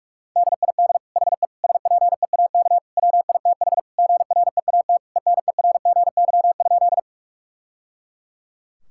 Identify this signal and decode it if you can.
Morse code, heard as a keyed tone, about 41 wpm, 700 Hz. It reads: DID HE SPEAK WITH GREAT ENERGY?
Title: Sentences from Top 1000 Words - Review Part 6 - 45wpm